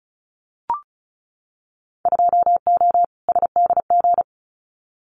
Morse code, heard as a keyed tone, about 36 wpm, 700 Hz. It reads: 2O SDG
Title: Winter Field Day Exchanges 35wpm